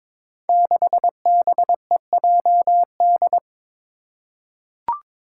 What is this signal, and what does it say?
Morse code, heard as a keyed tone, about 22 wpm, 700 Hz. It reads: 6BEJD
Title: Five Challenging Letter-Number Combinations 22wpm